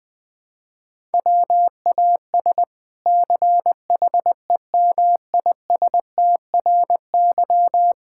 Morse code, heard as a keyed tone, about 20 wpm, 700 Hz. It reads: WAS CHEMISTRY
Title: Sentences from Top 1000 Words - Review Part 6 - 20wpm